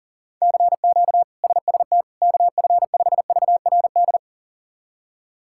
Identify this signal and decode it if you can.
Morse code, heard as a keyed tone, about 40 wpm, 700 Hz. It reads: CQ SST KF5VRD
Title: K1USN SST QSO Practice - 40wpm